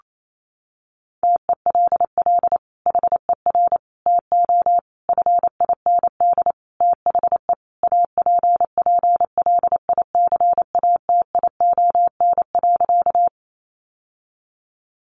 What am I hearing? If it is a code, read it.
Morse code, heard as a keyed tone, about 28 wpm, 700 Hz. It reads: TELL HER TO FIND THE APPLICATION.
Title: Sentences from Top 1200 Words 28wpm